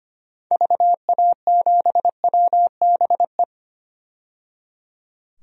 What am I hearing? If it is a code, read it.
Morse code, heard as a keyed tone, about 25 wpm, 700 Hz. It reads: VA7WBE
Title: DXCC-100 Call Signs - 6 Characters - 25wpm